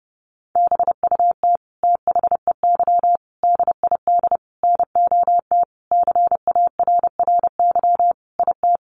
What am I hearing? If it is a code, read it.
Morse code, heard as a keyed tone, about 30 wpm, 700 Hz. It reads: BUT THEY DID NOT CARRY IT
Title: Sentences from Top 1400 Words 30wpm